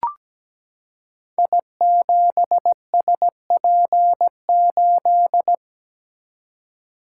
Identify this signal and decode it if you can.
Morse code, heard as a keyed tone, about 17 wpm, 690 Hz. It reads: I7SP8
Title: Code Groups - 5 Letter:Numbers - 17wpm